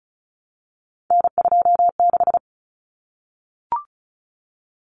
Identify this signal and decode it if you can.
Morse code, heard as a keyed tone, about 34 wpm, 700 Hz. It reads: N26
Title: Common Cut Numbers - 3-Digit 35wpm